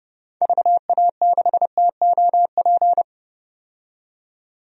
Morse code, heard as a keyed tone, about 29 wpm, 700 Hz. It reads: VA6TOP
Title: DXCC-100 Call Signs - 6 Characters - 30wpm